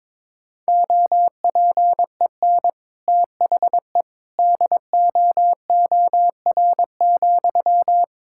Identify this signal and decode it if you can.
Morse code, heard as a keyed tone, about 22 wpm, 700 Hz. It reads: OPEN THE DOOR,
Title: Sentences from Top 1000 Words - Review Part 6 - 22wpm